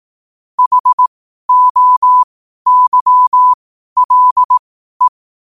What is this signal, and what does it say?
Morse code, heard as a keyed tone, about 18 wpm, spaced as for 12 wpm, 1000 Hz. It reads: HOYLE
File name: qotdfiles / 12wpm